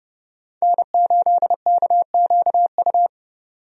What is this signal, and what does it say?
Morse code, heard as a keyed tone, about 31 wpm, 700 Hz. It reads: N8KQU
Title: USA Call Signs - 5 Characters - 30wpm